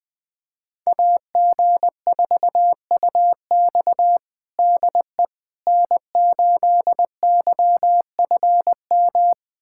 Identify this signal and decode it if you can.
Morse code, heard as a keyed tone, about 20 wpm, 700 Hz. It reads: AG4UX DE N8YFM
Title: USA Callsign DE Callsign - 5 Characters - 20wpm